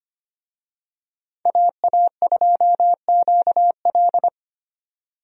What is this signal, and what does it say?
Morse code, heard as a keyed tone, about 25 wpm, 700 Hz. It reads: AA2QL
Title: USA Call Signs - 5 Characters - 25wpm